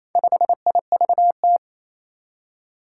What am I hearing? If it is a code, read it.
Morse code, heard as a keyed tone, about 28 wpm, 700 Hz. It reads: 5IVT
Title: Four Challenging Letter-Number Combinations 28wpm